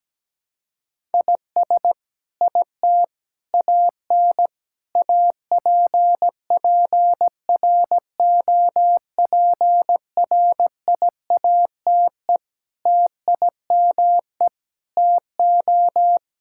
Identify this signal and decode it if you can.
Morse code, heard as a keyed tone, about 17 wpm, 700 Hz. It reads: IS IT AN APPROPRIATE TIME TO
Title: Sentences from Top 1000 Words - Review Part 5 - 17wpm